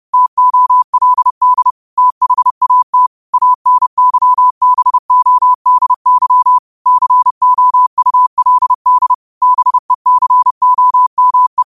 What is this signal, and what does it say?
Morse code, heard as a keyed tone, about 30 wpm, 1000 Hz. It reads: TOLD THAT ANYBODY COULD BECOME
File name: qotdfiles / 30wpm